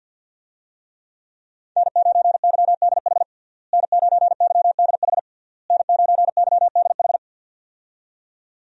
Morse code, heard as a keyed tone, about 49 wpm, 700 Hz. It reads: N9YDH N9YDH N9YDH
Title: USA Call Signs - 5 Characters - Repeated 3x - 50wpm